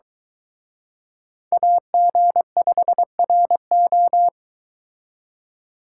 Morse code, heard as a keyed tone, about 24 wpm, 700 Hz. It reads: AG5RO